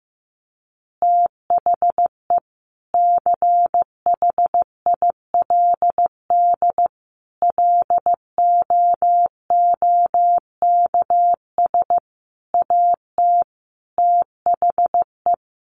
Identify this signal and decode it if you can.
Morse code, heard as a keyed tone, about 15 wpm, 700 Hz. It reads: THE CHILD LOOKS AT THE
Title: Additional Easy Phrases 15wpm